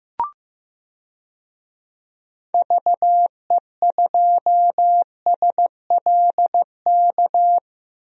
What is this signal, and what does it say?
Morse code, heard as a keyed tone, about 15 wpm, 700 Hz. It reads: VE2SLK
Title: DXCC-100 Call Signs - 6 Characters - 15wpm